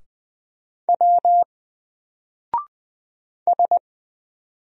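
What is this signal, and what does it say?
Morse code, heard as a keyed tone, about 20 wpm, 700 Hz. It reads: W S
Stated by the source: Single letter and number Morse drill at 20 wpm, from N4LCD.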